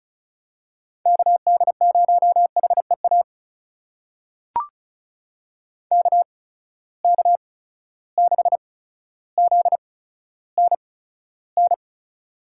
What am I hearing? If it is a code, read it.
Morse code, heard as a keyed tone, about 36 wpm, spaced as for 15 wpm, 700 Hz. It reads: KD0HEA KK6ZNN
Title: USA Call Signs - 6 Characters - 8x Character Spacing - 35wpm